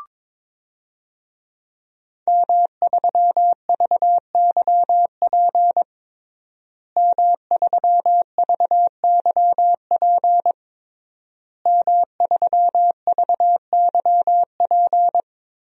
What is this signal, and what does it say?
Morse code, heard as a keyed tone, about 21 wpm, 700 Hz. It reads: M3VYP M3VYP M3VYP
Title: DXCC-100 Call Signs - 5 Characters - Repeated 3x - 22wpm